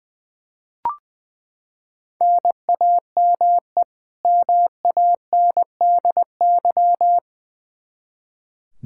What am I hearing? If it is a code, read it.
Morse code, heard as a keyed tone, about 21 wpm, 700 Hz. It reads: NAME MANDY
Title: Practice QSO - Exchange 1 - Beginning - 20wpm